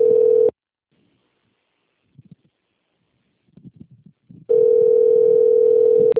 Ringback tone (North America), 440 + 480 Hz.
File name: out- - - - - .15781